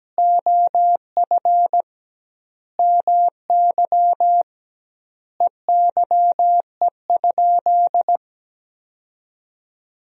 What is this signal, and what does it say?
Morse code, heard as a keyed tone, about 17 wpm, 700 Hz. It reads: OF MY EYE?
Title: 2x Word-Spacing - Sentences from Top 500 Words - 17wpm